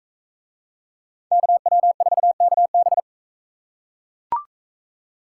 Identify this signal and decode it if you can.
Morse code, heard as a keyed tone, about 42 wpm, 700 Hz. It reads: KW4KB